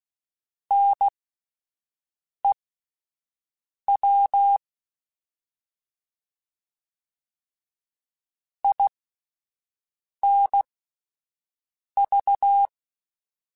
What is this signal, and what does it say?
Morse code, heard as a keyed tone, about 16 wpm, spaced as for 5 wpm, 800 Hz. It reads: NEW INV
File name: QSOs / QSOs-mp3s / WD0FBT 1@10